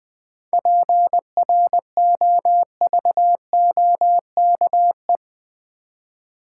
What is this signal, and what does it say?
Morse code, heard as a keyed tone, about 20 wpm, 700 Hz. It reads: PROVOKE
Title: 7 Letter Words - First 3 Letters Given - 20wpm